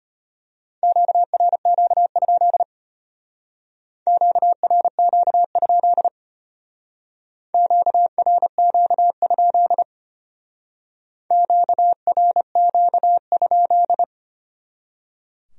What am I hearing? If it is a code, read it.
Morse code, heard as a keyed tone, about 32 wpm, 700 Hz. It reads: QRQ? QRQ? QRQ? QRQ?